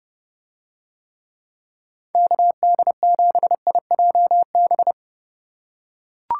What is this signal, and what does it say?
Morse code, heard as a keyed tone, about 29 wpm, 700 Hz. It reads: KD7IJB